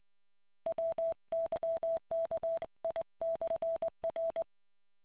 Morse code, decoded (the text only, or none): WYKI/R